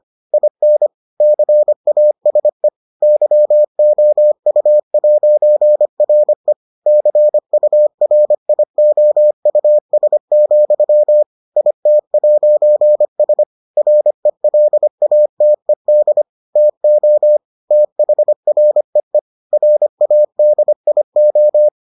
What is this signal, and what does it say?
Morse code, heard as a keyed tone, about 25 wpm, 600 Hz. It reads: IN CASE YOU'RE CURIOUS, IT'S RELATED TO THREE RADIO